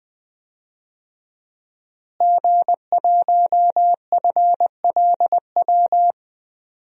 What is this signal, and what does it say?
Morse code, heard as a keyed tone, about 21 wpm, 700 Hz. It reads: G1FLW
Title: DXCC-100 Call Signs - 5 Characters - Repeated 3x - 20wpm